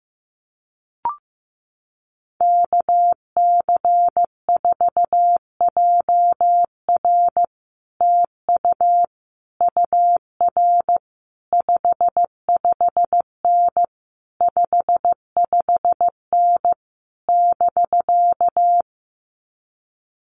Morse code, heard as a keyed tone, about 14 wpm, 700 Hz. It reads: KC4JR TU UR 55N 55N <BK>
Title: Common QSO Phrases Encore 15wpm